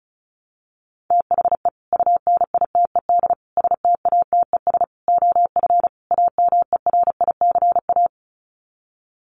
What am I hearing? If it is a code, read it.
Morse code, heard as a keyed tone, about 34 wpm, 700 Hz. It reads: THE UNITED STATES OF AMERICA
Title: Sets of 5 Words 35wpm